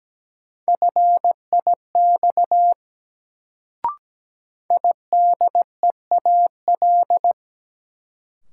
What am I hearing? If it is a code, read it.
Morse code, heard as a keyed tone, about 17 wpm, 700 Hz. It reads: FIX IDEAL
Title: Sentences from Top 1400 Words - New Word Drill Alt - 17wpm